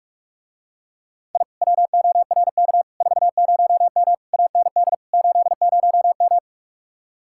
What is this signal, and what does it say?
Morse code, heard as a keyed tone, about 45 wpm, 700 Hz. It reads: I WORK 40M AND 80M